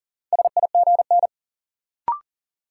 Morse code, heard as a keyed tone, about 37 wpm, 700 Hz. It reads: SIGN
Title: Sentences from Top 1000 Words - New Word Drill - 40wpm